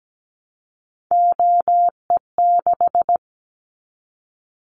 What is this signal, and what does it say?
Morse code, heard as a keyed tone, about 17 wpm, 700 Hz. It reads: OE6
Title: Callsign Prefixes for Top 25 DXCC - Repeated 3x - 17wpm